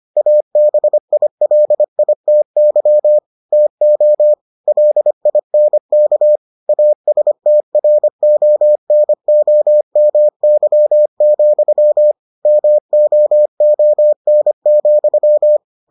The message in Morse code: ABILITY TO LINK ASTRONOMY, MOON,

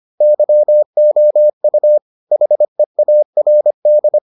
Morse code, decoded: YOU HEARD

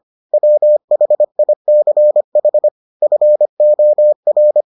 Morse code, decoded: WHICH FOR

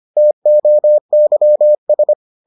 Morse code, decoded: TOYS

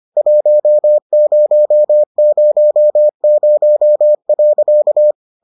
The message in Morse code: 1000.

